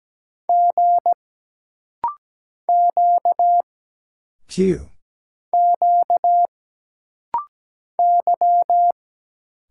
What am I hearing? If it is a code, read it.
Morse code, heard as a keyed tone, about 17 wpm, 700 Hz. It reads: G Q Q Y
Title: Single Letters - Rapid-Fire 17wpm